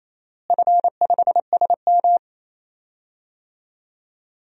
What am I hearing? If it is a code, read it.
Morse code, heard as a keyed tone, about 28 wpm, 700 Hz. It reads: F5SM
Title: DXCC-100 Call Signs - 4 Characters - Repeated 3x - 28wpm